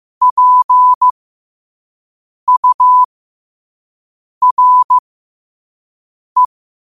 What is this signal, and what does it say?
Morse code, heard as a keyed tone, about 15 wpm, spaced as for 5 wpm, 1000 Hz. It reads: PURE
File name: qotdfiles / 5wpm